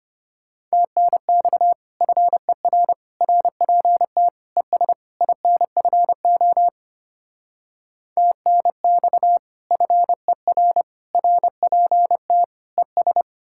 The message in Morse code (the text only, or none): TNX FER RPT ES INFO TNX FER RPT ES